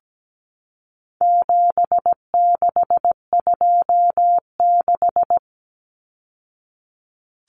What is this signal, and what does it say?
Morse code, heard as a keyed tone, about 17 wpm, 700 Hz. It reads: 7626
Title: Numbers - 4-Digit 17wpm